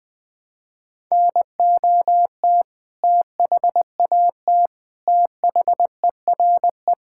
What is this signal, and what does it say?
Morse code, heard as a keyed tone, about 20 wpm, 700 Hz. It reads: NOT THAT THERE
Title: Sentences with 7-Word Phrases - Spoken Beginning - 20wpm